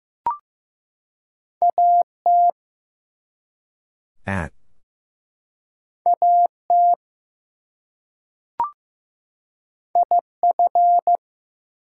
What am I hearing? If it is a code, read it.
Morse code, heard as a keyed tone, about 15 wpm, 700 Hz. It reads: AT AT IF
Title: Two Letter Words - 15wpm